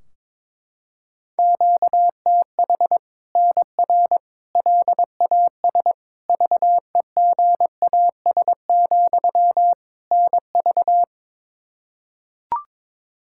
Morse code, heard as a keyed tone, about 22 wpm, 700 Hz. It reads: QTH NR LAS VEGAS, NV
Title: Practice QSO - Exchange 1 - Beginning - 22wpm